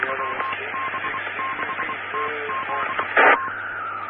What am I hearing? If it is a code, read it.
Morse code, heard as a keyed tone, about 24 wpm, spaced as for 21 wpm, 1060 Hz, with radio noise. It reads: WB6TZA
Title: Claremont, CA Police, Amateur Radio